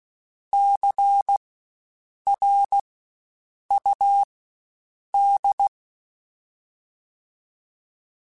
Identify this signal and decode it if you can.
Morse code, heard as a keyed tone, about 16 wpm, spaced as for 7 wpm, 800 Hz. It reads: CRUD
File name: Spaced Alphas / Spaced Alphas mp3s / Cs sp @15